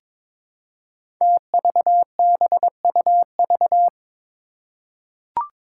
Morse code, heard as a keyed tone, about 21 wpm, 700 Hz. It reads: TVBUV
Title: Five Challenging Letter-Number Combinations 22wpm